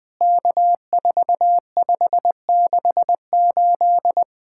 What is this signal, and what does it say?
Morse code, heard as a keyed tone, about 20 wpm, 700 Hz. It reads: K4568
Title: POTA QSO Practice - 20wpm